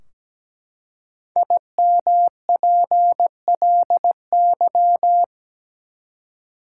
Morse code, heard as a keyed tone, about 17 wpm, 700 Hz. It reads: IMPLY
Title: Five Letter Words - 17wpm